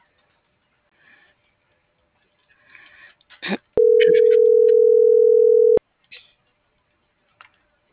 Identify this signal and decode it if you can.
Ringback tone (North America), 440 + 480 Hz.